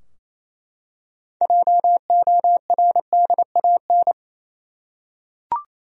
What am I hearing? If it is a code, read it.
Morse code, heard as a keyed tone, about 28 wpm, 700 Hz. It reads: JORDAN
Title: USA Common Names - 28wpm